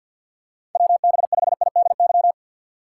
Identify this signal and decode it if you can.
Morse code, heard as a keyed tone, about 52 wpm, 700 Hz. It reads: WB5IDY
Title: K1USN SST QSO Practice - 50wpm